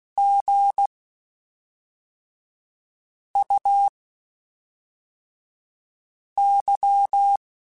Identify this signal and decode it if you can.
Morse code, heard as a keyed tone, about 16 wpm, spaced as for 3 wpm, 800 Hz. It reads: GUY
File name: Spaced / Chuckles Spaced-mp3s / HumanErr sp @5